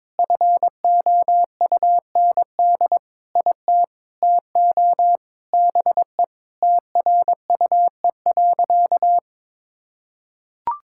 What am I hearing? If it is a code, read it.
Morse code, heard as a keyed tone, about 22 wpm, 700 Hz. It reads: FOUND IT TO BE TRUE.
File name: practice / 22wpm / Sentences with 4-Word Phrases - 22wpm